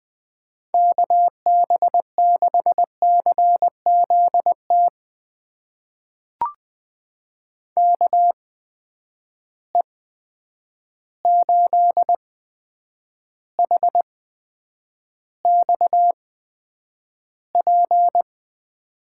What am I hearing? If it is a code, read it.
Morse code, heard as a keyed tone, about 21 wpm, spaced as for 9 wpm, 700 Hz. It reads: KB6CZT KE8HXP
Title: USA Call Signs - 6 Characters - 8x Character Spacing - 20wpm